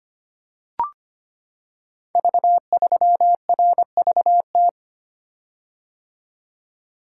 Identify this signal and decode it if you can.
Morse code, heard as a keyed tone, about 25 wpm, 700 Hz. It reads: V3RVT
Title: Code Groups - 5 Letter:Numbers - 25wpm